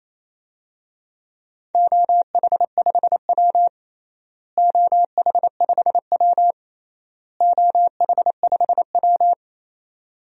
Morse code, heard as a keyed tone, about 28 wpm, 700 Hz. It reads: OH5W OH5W OH5W
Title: DXCC-100 Call Signs - 4 Characters - Repeated 3x - 28wpm